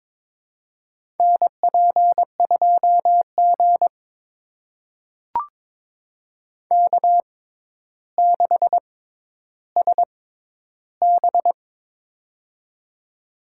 Morse code, heard as a keyed tone, about 22 wpm, spaced as for 11 wpm, 700 Hz. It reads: NP2G K6SB
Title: USA Call Signs - 4 Characters - 6x Character Spacing - 22wpm